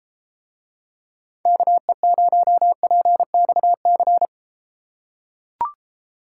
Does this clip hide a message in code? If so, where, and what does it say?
Yes: Morse code from 1.5 to 4.2 s, heard as a keyed tone, about 32 wpm, 700 Hz. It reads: KE0PXC